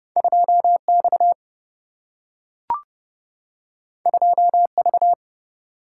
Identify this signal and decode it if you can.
Morse code, heard as a keyed tone, about 31 wpm, 700 Hz. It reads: 2 X 2 V
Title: Single-Letter Single-Number 30wpm